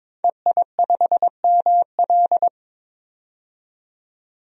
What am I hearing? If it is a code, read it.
Morse code, heard as a keyed tone, about 21 wpm, 700 Hz. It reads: EI5ML
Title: DXCC-100 Call Signs - 5 Characters - 22wpm